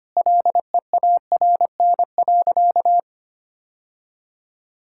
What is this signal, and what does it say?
Morse code, heard as a keyed tone, about 25 wpm, 700 Hz. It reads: LEARN.
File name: practice / 25wpm / Sentences with 4-Word Phrases - 25wpm